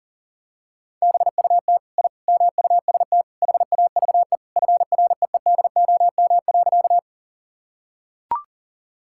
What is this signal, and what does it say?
Morse code, heard as a keyed tone, about 40 wpm, 700 Hz. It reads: BUT I MUST HAVE FREEDOM.
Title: Sentences from Top 1000 Words - Review Part 5 - 40wpm